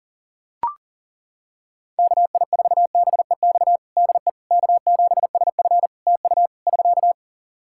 Morse code, heard as a keyed tone, about 40 wpm, 700 Hz. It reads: KI4BEX DE K7SF TU <SK>
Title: Practice QSO - Exchange 4 - Ending - 40wpm